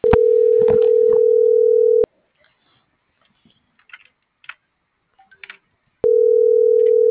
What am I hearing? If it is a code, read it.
Ringback tone (North America), 440 + 480 Hz.